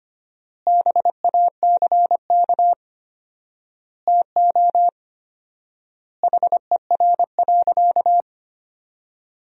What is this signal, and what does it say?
Morse code, heard as a keyed tone, about 25 wpm, 700 Hz. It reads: BACK TO HER.
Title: 4x Word-Spacing - Sentences from Top 300 Words - 25wpm